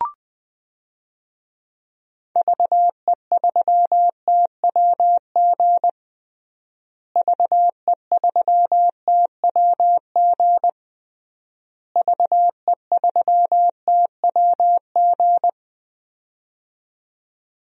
Morse code, heard as a keyed tone, about 20 wpm, 700 Hz. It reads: VE3TWG VE3TWG VE3TWG
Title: DXCC-100 Call Signs - 6 Characters - Repeated 3x - 20wpm